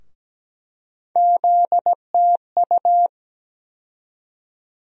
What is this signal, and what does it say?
Morse code, heard as a keyed tone, about 17 wpm, 700 Hz. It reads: ZTU
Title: USA Call Sign Postfixes - 3 Letters 17wpm